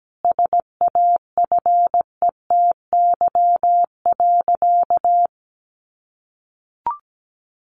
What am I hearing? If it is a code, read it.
Morse code, heard as a keyed tone, about 17 wpm, 700 Hz. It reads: SAFETY.
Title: Sentences from Top 700 Words 17wpm